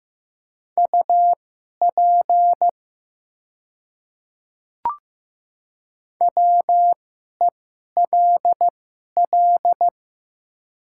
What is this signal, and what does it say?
Morse code, heard as a keyed tone, about 15 wpm, spaced as for 11 wpm, 700 Hz. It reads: UP WELL
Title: 2x Character Spacing - Top 200 Words - 15wpm